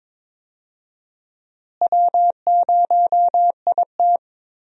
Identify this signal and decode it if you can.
Morse code, heard as a keyed tone, about 21 wpm, 700 Hz. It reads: W0IT
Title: USA Call Signs - 4 Characters - Repeated 2x - 22wpm